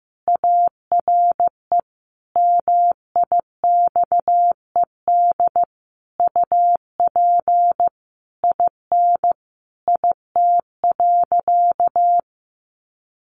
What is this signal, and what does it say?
Morse code, heard as a keyed tone, about 15 wpm, 700 Hz. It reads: ARE MIXED UP IN IT.
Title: Sentences from Top 1500 Words 15wpm